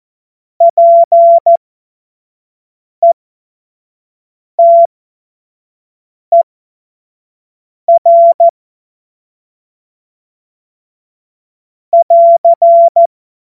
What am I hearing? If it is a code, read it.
Morse code, heard as a keyed tone, about 14 wpm, spaced as for 5 wpm, 680 Hz. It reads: PETER <AR>
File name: Morsekurs / Witze / Witz 31.Oktober 2023 25 F70